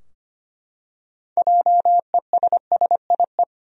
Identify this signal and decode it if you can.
Morse code, heard as a keyed tone, about 25 wpm, 700 Hz. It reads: JESSIE
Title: USA Common Names - 25wpm